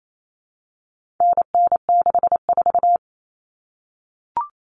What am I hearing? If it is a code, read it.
Morse code, heard as a keyed tone, about 28 wpm, 700 Hz. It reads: NN64
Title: Common Cut Numbers - 4-Digit 28wpm